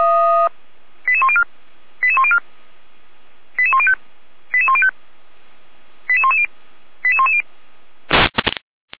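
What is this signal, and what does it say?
Five-tone selective call (ZVEI), heard as quick runs of five tones, tones of about 70 ms. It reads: 88104 88104 88105 88105 88110 88110
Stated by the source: ZVEI five-tone selective call received on an SDR.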